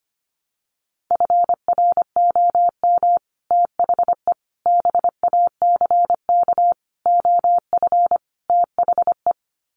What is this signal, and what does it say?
Morse code, heard as a keyed tone, about 25 wpm, 700 Hz. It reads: FROM THE BACK OF THE
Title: Sets of 5 Words Encore 25wpm